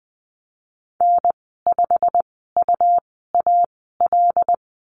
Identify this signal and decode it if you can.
Morse code, heard as a keyed tone, about 19 wpm, spaced as for 14 wpm, 700 Hz. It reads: N5UAL
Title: 122 Lesson 20g - Number 3 - Call Signs 20wpm